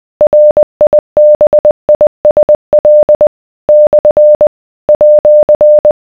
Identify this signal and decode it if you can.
Morse code, heard as a keyed tone, about 20 wpm, 600 Hz. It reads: RIBISL / @